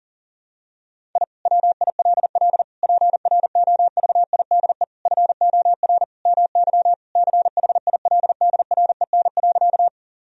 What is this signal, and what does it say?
Morse code, heard as a keyed tone, about 40 wpm, 700 Hz. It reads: I WILL PROVIDE FOR MY CHILDREN.
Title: Sentences from Top 1000 Words - Review Part 3 - 40wpm